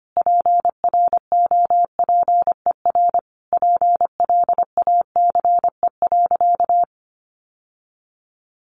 Morse code, heard as a keyed tone, about 25 wpm, 700 Hz. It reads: PROPER PLACE.
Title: Sentences from Top 800 Words 25wpm